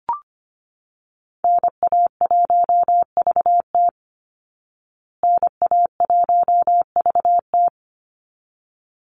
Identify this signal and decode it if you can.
Morse code, heard as a keyed tone, about 25 wpm, 700 Hz. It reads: NA1VT NA1VT
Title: Call Signs Repeated 3x - 25wpm